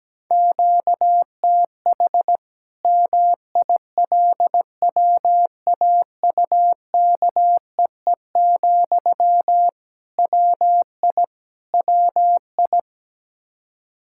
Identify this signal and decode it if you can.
Morse code, heard as a keyed tone, about 17 wpm, 700 Hz. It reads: QTH MILWAUKEE, WI WI
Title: K1USN SST QSO Practice 17wpm